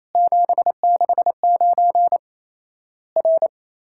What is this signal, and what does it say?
Morse code, heard as a keyed tone, about 28 wpm, 700 Hz. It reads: 769 R
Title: Worked All Europe DX Contest - QTC Exchange - 28wpm